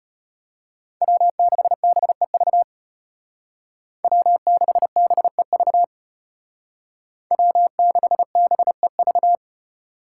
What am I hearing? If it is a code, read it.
Morse code, heard as a keyed tone, about 32 wpm, 700 Hz. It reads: W6BEV W6BEV W6BEV